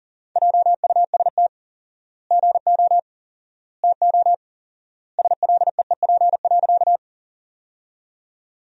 Morse code, heard as a keyed tone, about 40 wpm, 700 Hz. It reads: JUST GO TO SLEEP.
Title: 4x Word-Spacing - Sentences from Top 500 Words - 40wpm